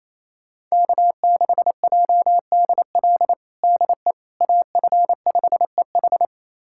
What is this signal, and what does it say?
Morse code, heard as a keyed tone, about 28 wpm, 700 Hz. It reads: K6JDL DE AF5EH
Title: USA Callsign DE Callsign - 5 Characters - 28wpm